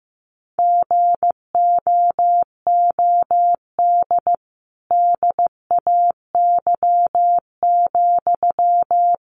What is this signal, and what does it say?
Morse code, heard as a keyed tone, about 14 wpm, 700 Hz. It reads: GOOD DAY,
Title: Sentences from Top 1400 Words 15wpm